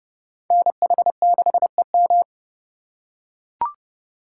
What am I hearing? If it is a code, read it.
Morse code, heard as a keyed tone, about 29 wpm, 700 Hz. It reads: NH6EM